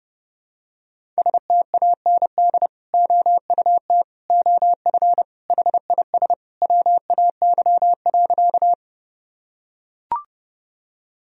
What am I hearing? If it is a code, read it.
Morse code, heard as a keyed tone, about 30 wpm, 700 Hz. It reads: STAND OUT OF HIS WAY.
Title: Sentences from Top 1000 Words - Review Part 1 - 30wpm